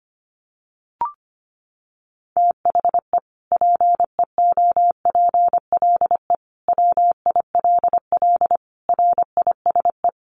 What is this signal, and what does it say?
Morse code, heard as a keyed tone, about 25 wpm, 700 Hz. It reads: THE PEOPLE WILL RISE
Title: Sentences from Top 900 Words 25wpm